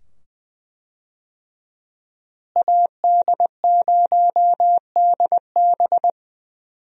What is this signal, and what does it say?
Morse code, heard as a keyed tone, about 20 wpm, 700 Hz. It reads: AD0DB
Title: USA Call Signs - 5 Characters - Repeated 2x - 20wpm